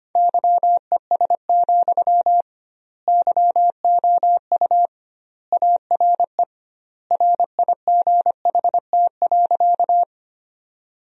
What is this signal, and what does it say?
Morse code, heard as a keyed tone, about 25 wpm, 700 Hz. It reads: YES, YOU ARE RIGHT.
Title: 2x Word-Spacing - Sentences from Top 500 Words - 25wpm